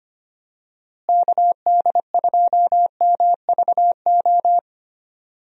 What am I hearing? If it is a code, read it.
Morse code, heard as a keyed tone, about 25 wpm, 700 Hz. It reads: KD2MVO
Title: USA Call Signs - 6 Characters - Repeated 2x - 25wpm